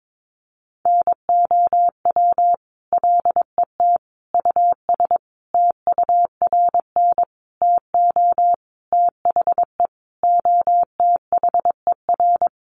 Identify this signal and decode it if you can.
Morse code, heard as a keyed tone, about 22 wpm, 700 Hz. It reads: NOW LET US TURN TO THE OTHER
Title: Sentences from Top 1400 Words 22wpm